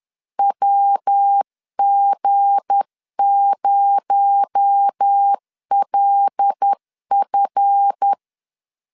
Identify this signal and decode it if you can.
Morse code, heard as a keyed tone, about 10 wpm, 800 Hz. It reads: WG0LF
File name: LEZIONI / 7 GF9Z / 7 GF 10wpm 2514